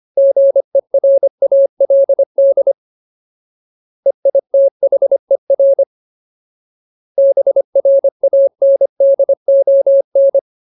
Morse code, heard as a keyed tone, about 25 wpm, 550 Hz. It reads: GERALD EITHER BRANDON
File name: Ultimate Word List / 25WPM Ultimate